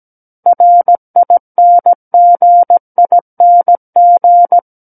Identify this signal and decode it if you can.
Morse code, heard as a keyed tone, about 17 wpm, 700 Hz. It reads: RINGING
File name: Book One / 08-Chapter 8 - 17 WPM